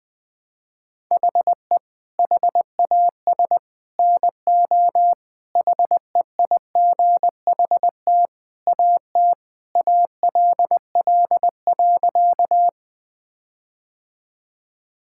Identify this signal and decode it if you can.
Morse code, heard as a keyed tone, about 20 wpm, 700 Hz. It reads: HE HAS NO HEIGHT AT ALL.
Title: Sentences from Top 1600 Words 20wpm